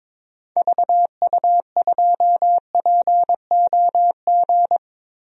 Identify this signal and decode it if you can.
Morse code, heard as a keyed tone, about 22 wpm, 700 Hz. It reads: VU2POG
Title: DXCC-100 Call Signs - 6 Characters - Repeated 3x - 22wpm